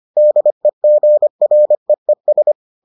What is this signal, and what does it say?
Morse code, heard as a keyed tone, about 25 wpm, 600 Hz. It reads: DEGREES